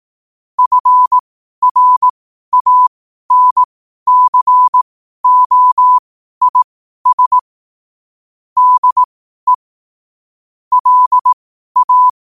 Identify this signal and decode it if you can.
Morse code, heard as a keyed tone, about 18 wpm, spaced as for 12 wpm, 1000 Hz. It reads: FRANCOIS DE LA
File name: qotdfiles / 12wpm